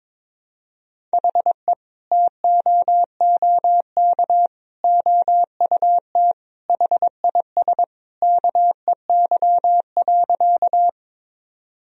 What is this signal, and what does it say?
Morse code, heard as a keyed tone, about 22 wpm, 700 Hz. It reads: HE TOOK OUT HIS KEY.
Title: Sentences from Top 1000 Words - Review Part 4 - 22wpm